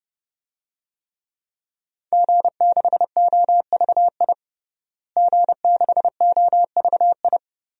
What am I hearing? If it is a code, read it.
Morse code, heard as a keyed tone, about 30 wpm, 700 Hz. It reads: G6OVI G6OVI
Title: DXCC-100 Call Signs - 5 Characters - Repeated 2x - 30wpm